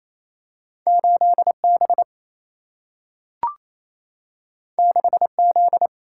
Morse code, heard as a keyed tone, about 28 wpm, 700 Hz. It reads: 8 B 6 Z
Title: Single-Letter Single-Number 28wpm